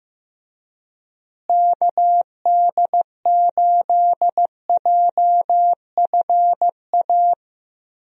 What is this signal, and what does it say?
Morse code, heard as a keyed tone, about 15 wpm, 700 Hz. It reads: KD8JFA
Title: USA Call Signs - 6 Characters - Repeated 3x - 15wpm